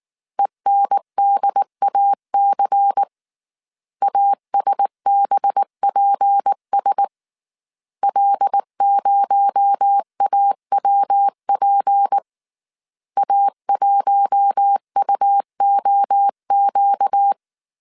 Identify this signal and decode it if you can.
Morse code, heard as a keyed tone, about 19 wpm, 800 Hz. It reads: ENDAC ASBPS L0AWP A1UOQ
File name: CW / DET / 1 6 1500C S A 18wpm 2255